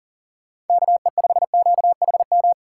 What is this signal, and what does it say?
Morse code, heard as a keyed tone, about 51 wpm, 700 Hz. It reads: KE5QHM